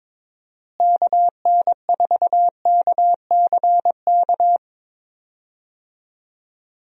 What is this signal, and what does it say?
Morse code, heard as a keyed tone, about 22 wpm, 700 Hz. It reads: KN4KCK
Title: USA Call Signs - 6 Characters - Repeated 2x - 22wpm